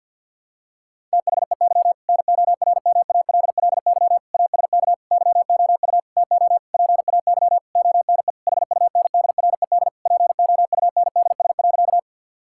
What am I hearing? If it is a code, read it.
Morse code, heard as a keyed tone, about 46 wpm, 700 Hz. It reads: THEY NORMALLY ASK YOU TO PAY ONE HUNDRED POUNDS.
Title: Sentences from Top 1000 Words - Review Part 6 - 50wpm